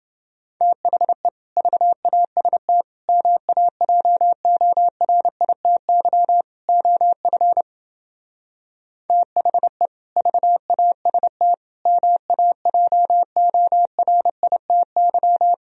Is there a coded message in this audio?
Yes: Morse code from 0.6 to 15.5 s, heard as a keyed tone, about 28 wpm, 700 Hz. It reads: THE VAST MAJORITY OF THE VAST MAJORITY